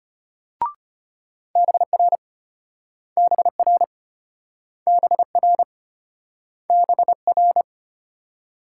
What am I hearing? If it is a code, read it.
Morse code, heard as a keyed tone, about 33 wpm, 700 Hz. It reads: BR BR BR BR